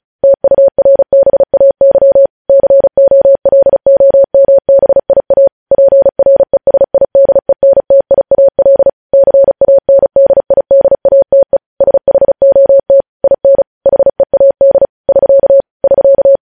Morse code, heard as a keyed tone, about 36 wpm, 550 Hz. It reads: TURBAY COLOMBIA PRESIDENTIAL CANDIDATE SHOT IN HEAD <SK> <SK>
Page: news headlines in morse code at 35 wpm